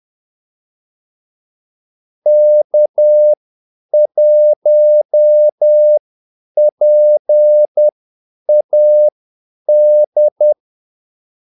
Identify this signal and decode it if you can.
Morse code, heard as a keyed tone, about 10 wpm, spaced as for 8 wpm, 600 Hz. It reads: K1PAD
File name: Callsigns-10-8wpm